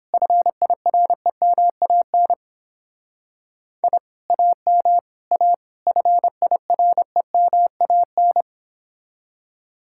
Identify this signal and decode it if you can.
Morse code, heard as a keyed tone, about 28 wpm, 700 Hz. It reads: FIREMAN I AM A FIREMAN